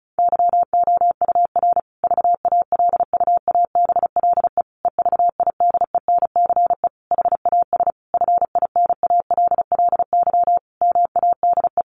Morse code, heard as a keyed tone, about 34 wpm, 700 Hz. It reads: YOUR VALUABLE EVIDENCE HAS FINALLY MADE
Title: Sentences from Top 1300 Words 35wpm